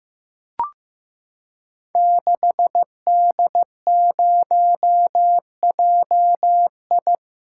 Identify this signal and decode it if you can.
Morse code, heard as a keyed tone, about 15 wpm, 700 Hz. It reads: 6D0JI
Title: Code Groups - 5 Letter:Numbers - 15wpm